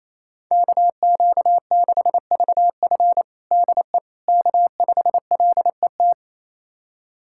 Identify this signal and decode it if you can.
Morse code, heard as a keyed tone, about 27 wpm, 700 Hz. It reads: KQ6VF DE K5LET
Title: USA Callsign DE Callsign - 5 Characters - 28wpm